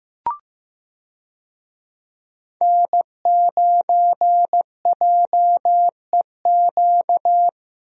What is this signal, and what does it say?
Morse code, heard as a keyed tone, about 15 wpm, 700 Hz. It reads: N9JEQ
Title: USA Call Signs - 5 Characters - Repeated 3x - 15wpm